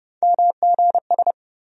Morse code, heard as a keyed tone, about 30 wpm, 700 Hz. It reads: MGS
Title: USA Call Sign Postfixes - 3 Letters 30wpm